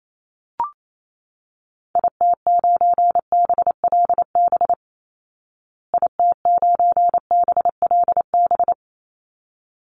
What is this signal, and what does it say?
Morse code, heard as a keyed tone, about 28 wpm, 690 Hz. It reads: IT9BLB IT9BLB
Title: Call Signs Repeated 3x - 28wpm